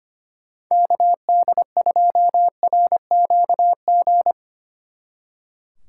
Morse code, heard as a keyed tone, about 32 wpm, 700 Hz. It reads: KD2RQG